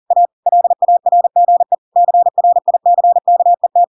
Morse code, heard as a keyed tone, about 40 wpm, 700 Hz. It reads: A LARGE CRICKET
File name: Pinocchio / Pinocchio ch4 40 40 ik1hge